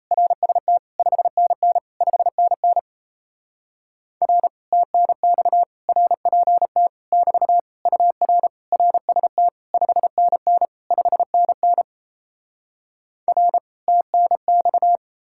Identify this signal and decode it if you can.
Morse code, heard as a keyed tone, about 38 wpm, 700 Hz. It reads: RST 5NN 5NN R TNX RPT <BT> UR RST 5NN 5NN R TNX